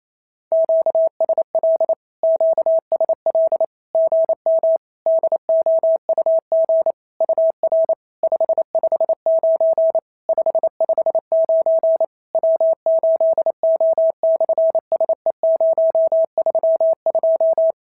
Morse code, heard as a keyed tone, about 27 wpm, 650 Hz. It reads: QSL QSL GM DOUG UR 559 559 W8O/SE032
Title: SOTA QSO Practice 28wpm